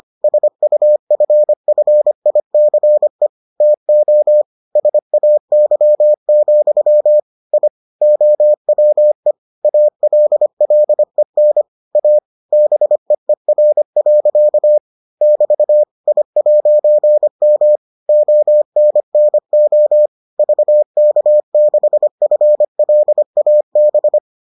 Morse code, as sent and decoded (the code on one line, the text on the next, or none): ... ..- ..-. ..-. .. -.-. / - --- / ... .- -.-- --..-- / .. / --- .-- . / .- .-.. .-.. -. / .- / -... . .-. .-.-.- / -...- / .. .----. -- / --- -. -. --- / ...- -.- -.... ..-. .-.. .- -...
SUFFICE TO SAY, I OWE ALLEN A BEER. <BT> I'M ONNO VK6FLAB